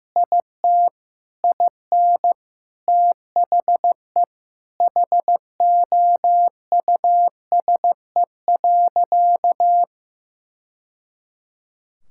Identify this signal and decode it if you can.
Morse code, heard as a keyed tone, about 16 wpm, 700 Hz. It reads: IT IN THE HOUSE.
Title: Sentences from Top 1000 Words - Review Part 5 - 15wpm